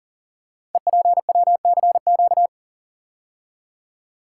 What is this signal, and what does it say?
Morse code, heard as a keyed tone, about 40 wpm, 690 Hz. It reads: EPWCQ
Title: Code Groups - 5 Letter:Numbers - 40wpm